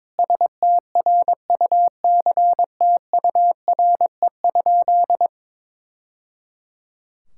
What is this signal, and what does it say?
Morse code, heard as a keyed tone, about 22 wpm, 700 Hz. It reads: STRUCTURE?
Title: Sentences from Top 1000 Words - Review Part 6 - 22wpm